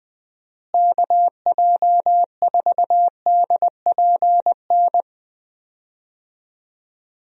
Morse code, heard as a keyed tone, about 20 wpm, 700 Hz. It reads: KJ4DPN
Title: USA Call Signs - 6 Characters - Repeated 3x - 20wpm